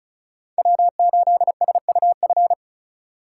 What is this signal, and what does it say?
Morse code, heard as a keyed tone, about 34 wpm, 700 Hz. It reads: W8SUF
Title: USA Call Signs - 5 Characters - 4x Character Spacing - 35wpm